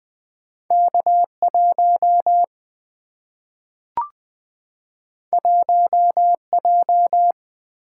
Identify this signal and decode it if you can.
Morse code, heard as a keyed tone, about 20 wpm, 700 Hz. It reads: K 1 1 J
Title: Two Letter-Number Combinations - 20wpm